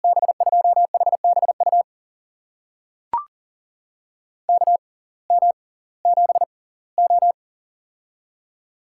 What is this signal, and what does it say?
Morse code, heard as a keyed tone, about 40 wpm, spaced as for 22 wpm, 700 Hz. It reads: B2HBU KM7O
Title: USA Call Signs - 6x Character Spacing - 40wpm